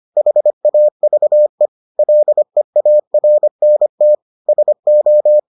Morse code, heard as a keyed tone, about 25 wpm, 600 Hz. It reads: HAVE LEARNT SO